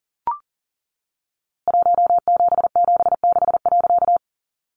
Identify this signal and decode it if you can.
Morse code, heard as a keyed tone, about 40 wpm, 700 Hz. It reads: 1776.
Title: A Sound Of Thunder 40wpm